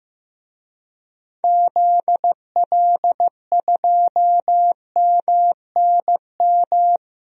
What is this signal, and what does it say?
Morse code, heard as a keyed tone, about 15 wpm, 700 Hz. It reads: ZL2MNM
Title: DXCC-100 Call Signs - 6 Characters - 15wpm